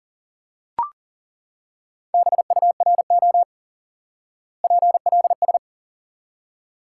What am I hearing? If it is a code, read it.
Morse code, heard as a keyed tone, about 40 wpm, 700 Hz. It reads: BURO PLS
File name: IQE Course 3 - Lesson 89 - YAGI - Two Elements - 40wpm